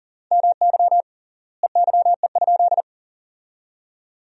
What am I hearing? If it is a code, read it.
Morse code, heard as a keyed tone, about 40 wpm, 700 Hz. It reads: MY EYE?
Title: 3x Word-Spacing - Sentences from Top 500 Words - 40wpm